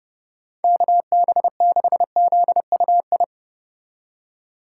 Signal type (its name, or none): Morse code